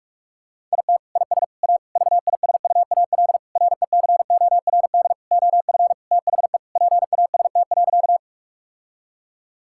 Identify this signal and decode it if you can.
Morse code, heard as a keyed tone, about 46 wpm, 700 Hz. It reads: IT IS A VISUAL RECORD OF THE PAST.
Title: Sentences from Top 1000 Words - Review Part 6 - 45wpm